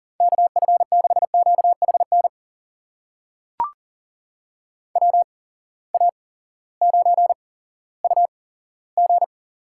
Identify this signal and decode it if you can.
Morse code, heard as a keyed tone, about 42 wpm, spaced as for 19 wpm, 700 Hz. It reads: KF6QHN WA9UG
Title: USA Call Signs - 8x Character Spacing - 40wpm